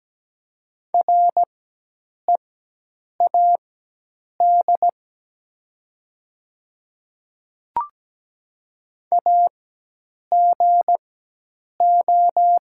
Morse code, heard as a keyed tone, about 17 wpm, spaced as for 8 wpm, 700 Hz. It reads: READ AGO
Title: 4x Character Spacing - Top 500 Words - 17wpm